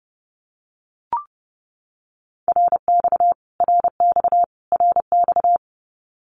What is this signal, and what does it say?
Morse code, heard as a keyed tone, about 30 wpm, 700 Hz. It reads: RX RX RX
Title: Abbreviations Repeated 3x - 30wpm